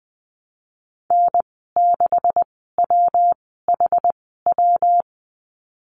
Morse code, heard as a keyed tone, about 20 wpm, spaced as for 15 wpm, 700 Hz. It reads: N6WHW
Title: 167 Lesson 27g - Number 7 - Call Signs 20wpm